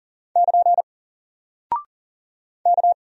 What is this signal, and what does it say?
Morse code, heard as a keyed tone, about 40 wpm, 700 Hz. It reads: <KN> K
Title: Single Character (Letter:Number:Punctuation:Prosign) 40wpm